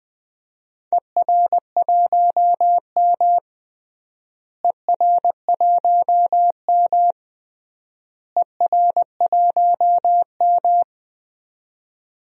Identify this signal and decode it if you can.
Morse code, heard as a keyed tone, about 20 wpm, 700 Hz. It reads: ER1M ER1M ER1M
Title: DXCC-100 Call Signs - 4 Characters - Repeated 3x - 20wpm